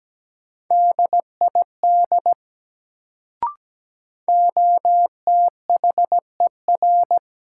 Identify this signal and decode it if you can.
Morse code, heard as a keyed tone, about 17 wpm, 700 Hz. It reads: DID OTHER
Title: Sentences from Top 100 Words - New Word Drill - 17wpm